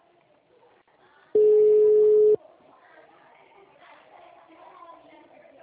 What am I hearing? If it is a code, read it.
Ringback tone (Europe, 425 Hz), with speech.
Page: Ribeirao sala 906